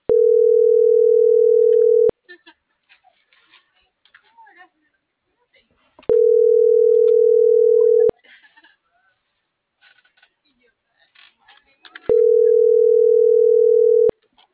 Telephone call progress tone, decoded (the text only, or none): ringback tone